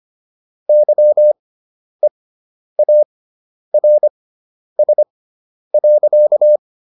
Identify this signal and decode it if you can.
Morse code, heard as a keyed tone, about 25 wpm, spaced as for 10 wpm, 600 Hz. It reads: YEARS.